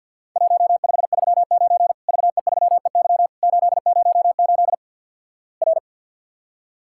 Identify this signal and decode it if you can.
Morse code, heard as a keyed tone, about 50 wpm, 700 Hz. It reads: 1539 VE3EY 808 R
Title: Worked All Europe DX Contest - QTC Exchange - 50wpm